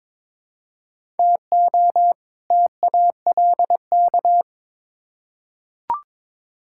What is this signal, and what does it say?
Morse code, heard as a keyed tone, about 22 wpm, 700 Hz. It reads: TO TALK
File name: Build A Phrase - 2-Word Phrases - 22wpm